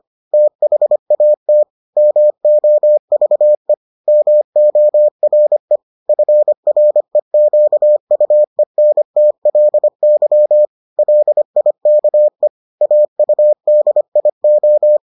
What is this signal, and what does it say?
Morse code, heard as a keyed tone, about 25 wpm, 600 Hz. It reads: THAT MOVE MORE FREQUENTLY LIKE AUDIO